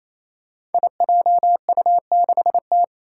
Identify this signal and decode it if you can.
Morse code, heard as a keyed tone, about 27 wpm, 700 Hz. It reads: IJU6T
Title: Five Challenging Letter-Number Combinations 28wpm